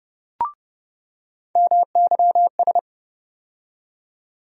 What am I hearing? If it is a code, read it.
Morse code, heard as a keyed tone, about 30 wpm, 700 Hz. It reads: MYS
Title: USA Call Sign Postfixes - 3 Letters 30wpm